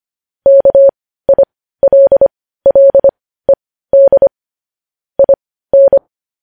Morse code, heard as a keyed tone, about 25 wpm, spaced as for 15 wpm, 550 Hz. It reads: KILLED IN
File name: podcast / News headlines-1x-25x15